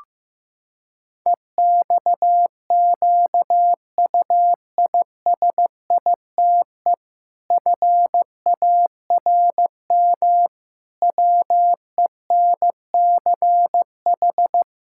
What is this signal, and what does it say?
Morse code, heard as a keyed tone, about 15 wpm, 700 Hz. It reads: EXQUISITE FARM WENCH
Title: Pangram Challenge v2 - 15wpm